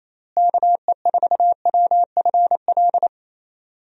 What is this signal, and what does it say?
Morse code, heard as a keyed tone, about 28 wpm, 700 Hz. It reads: KE4WFL
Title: USA Call Signs - 6 Characters - Repeated 3x - 28wpm